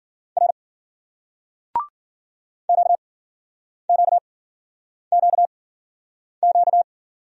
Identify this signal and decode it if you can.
Morse code, heard as a keyed tone, about 52 wpm, spaced as for 9 wpm, 700 Hz. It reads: R QQQQ